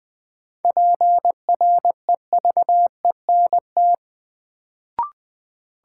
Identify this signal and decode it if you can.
Morse code, heard as a keyed tone, about 20 wpm, 700 Hz. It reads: PREVENT
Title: Sentences from Top 1100 Words - New Word Drill Alt - 20wpm